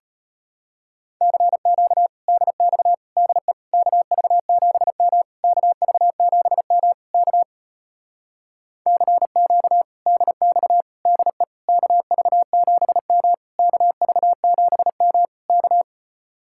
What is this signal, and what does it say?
Morse code, heard as a keyed tone, about 33 wpm, 700 Hz. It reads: CQ DX DE KV7M KV7M K CQ DX DE KV7M KV7M K